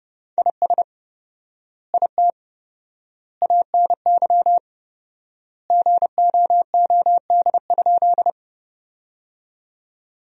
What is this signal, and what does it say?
Morse code, heard as a keyed tone, about 30 wpm, 700 Hz. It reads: IS IT ANY GOOD?
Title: 4x Word-Spacing - Sentences from Top 200 Words - 30wpm